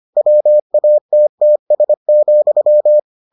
Morse code, heard as a keyed tone, about 25 wpm, 600 Hz. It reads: WATTS,